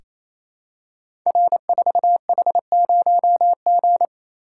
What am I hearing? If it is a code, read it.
Morse code, heard as a keyed tone, about 28 wpm, 700 Hz. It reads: R4H0G
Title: Code Groups - 5 Letter:Numbers - 28wpm